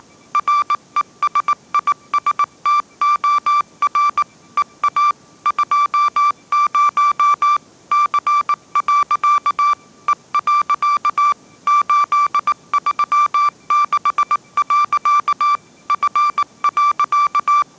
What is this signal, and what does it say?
Morse code, heard as a keyed tone, about 19 wpm, 1170 Hz. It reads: RESISTOR EA 20 C. E. 836. F.
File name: 20wpm